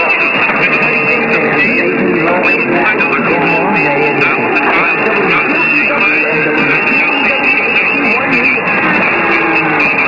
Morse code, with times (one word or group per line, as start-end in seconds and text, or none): none